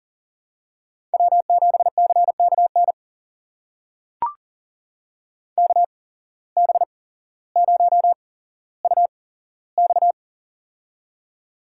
Morse code, heard as a keyed tone, about 40 wpm, spaced as for 17 wpm, 700 Hz. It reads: W7CKN KB0UX
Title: USA Call Signs - 5 Characters - 8x Character Spacing - 40wpm